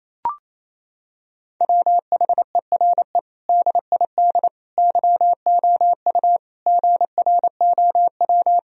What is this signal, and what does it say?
Morse code, heard as a keyed tone, about 28 wpm, 700 Hz. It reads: WHERE DID YOU GROW
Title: Sentences from Top 1000 Words - Review Part 1 - 28wpm